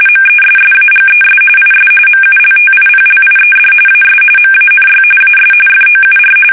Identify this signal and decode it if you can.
RTTY, about 45.45 baud. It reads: MESSAGES IS MULTIPLIED AS MANY AS FOUR
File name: fsk850